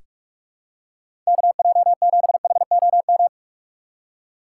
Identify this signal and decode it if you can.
Morse code, heard as a keyed tone, about 47 wpm, 700 Hz. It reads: KJ7HOM
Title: USA Call Signs - 2x Character Spacing - 45wpm